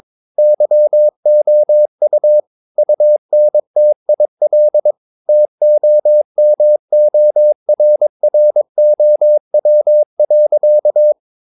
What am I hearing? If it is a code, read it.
Morse code, heard as a keyed tone, about 22 wpm, 600 Hz. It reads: YOU UNTIL TOMORROW.
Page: Romanen Never af Ken Follet på engelsk.